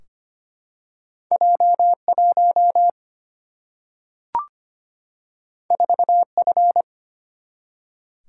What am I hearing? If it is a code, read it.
Morse code, heard as a keyed tone, about 25 wpm, 700 Hz. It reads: J 1 4 F
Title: Single-Letter Single-Number 25wpm